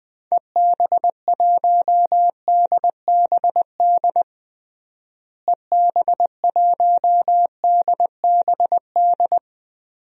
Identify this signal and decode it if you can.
Morse code, heard as a keyed tone, about 20 wpm, 700 Hz. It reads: EB1DBD EB1DBD
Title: DXCC-100 Call Signs - 6 Characters - Repeated 3x - 20wpm